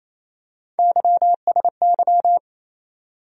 Morse code, heard as a keyed tone, about 28 wpm, 700 Hz. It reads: YSY
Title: USA Call Sign Postfixes - 3 Letters 28wpm